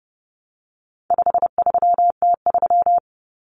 Morse code, heard as a keyed tone, about 30 wpm, 700 Hz. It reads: 53T3
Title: Common Cut Numbers - 4-Digit 30wpm